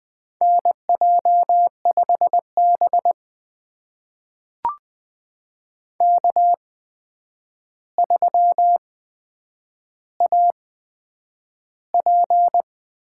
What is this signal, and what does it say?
Morse code, heard as a keyed tone, about 20 wpm, spaced as for 9 wpm, 700 Hz. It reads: NJ5B K3AP
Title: USA Call Signs - 8x Character Spacing - 20wpm